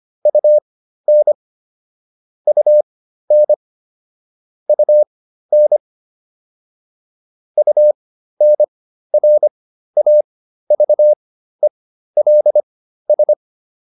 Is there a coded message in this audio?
Yes: Morse code from 0.3 to 13.3 s, heard as a keyed tone, about 25 wpm, spaced as for 13 wpm, 600 Hz. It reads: UN UN UN UNRAVELS